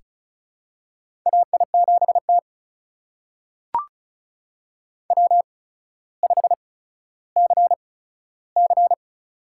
Morse code, heard as a keyed tone, about 36 wpm, spaced as for 15 wpm, 700 Hz. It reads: AI7T W5CC
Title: USA Call Signs - 4 Characters - 8x Character Spacing - 35wpm